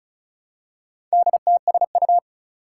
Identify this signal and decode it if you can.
Morse code, heard as a keyed tone, about 36 wpm, 700 Hz. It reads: DTSU
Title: Four Challenging Letter-Number Combinations 35wpm